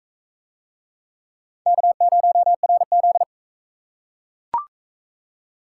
Morse code, heard as a keyed tone, about 42 wpm, 700 Hz. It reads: K0RZ